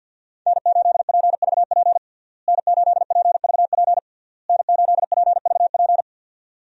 Morse code, heard as a keyed tone, about 51 wpm, 700 Hz. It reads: N8PVP N8PVP N8PVP
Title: USA Call Signs - 5 Characters - Repeated 3x - 50wpm